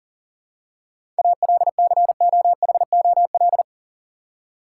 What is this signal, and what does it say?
Morse code, heard as a keyed tone, about 40 wpm, 700 Hz. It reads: ALCOHOL
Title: Seven Letter Words - 40wpm